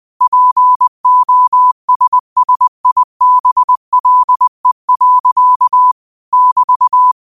Morse code, heard as a keyed tone, about 20 wpm, 1000 Hz. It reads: POSSIBLE. <BT>